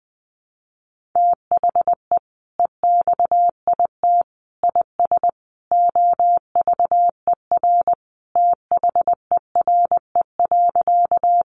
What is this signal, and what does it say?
Morse code, heard as a keyed tone, about 19 wpm, 700 Hz. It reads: THE EXIT IS OVER THERE.
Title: Sentences from Top 1100 Words 20wpm